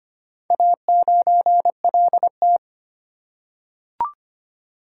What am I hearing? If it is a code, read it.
Morse code, heard as a keyed tone, about 25 wpm, 700 Hz. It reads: A9LT
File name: practice / 25wpm / Four Letter-Number Combinations - 25wpm